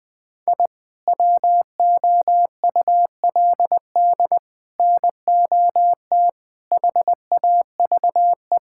Morse code, heard as a keyed tone, about 20 wpm, 700 Hz. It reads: I WOULD NOT HAVE
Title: Sentences with 6-Word Phrases - 20wpm